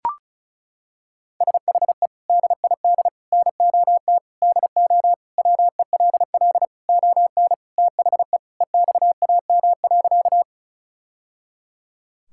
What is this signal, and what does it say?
Morse code, heard as a keyed tone, about 34 wpm, 700 Hz. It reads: SHE DID NOT DO WELL ON THE EXAM.
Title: Sentences from Top 1000 Words - Review Part 5 - 35wpm